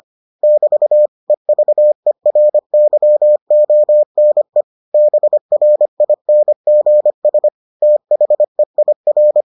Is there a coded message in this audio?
Yes: Morse code from 0.4 to 9.4 s, heard as a keyed tone, about 25 wpm, 600 Hz. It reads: <BT> EVERYONE BRINGS THEIR